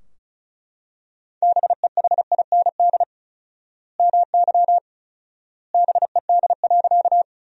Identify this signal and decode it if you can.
Morse code, heard as a keyed tone, about 36 wpm, 700 Hz. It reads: BEHIND MY BED.
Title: 4x Word-Spacing - Sentences from Top 500 Words - 35wpm